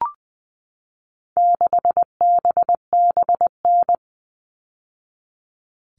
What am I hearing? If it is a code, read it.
Morse code, heard as a keyed tone, about 20 wpm, 700 Hz. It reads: 6BBN
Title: All Cut Numbers - 4-Digit 20wpm